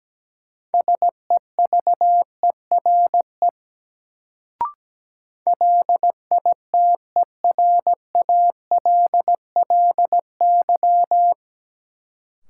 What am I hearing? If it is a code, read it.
Morse code, heard as a keyed tone, about 17 wpm, 700 Hz. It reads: SEVERE LITERALLY
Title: Sentences from Top 1400 Words - New Word Drill Alt - 17wpm